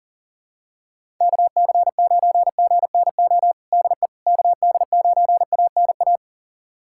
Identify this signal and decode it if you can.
Morse code, heard as a keyed tone, about 40 wpm, 700 Hz. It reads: KC9GNO DE KD9ANA
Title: USA Callsign DE Callsign - 6 Characters - 40wpm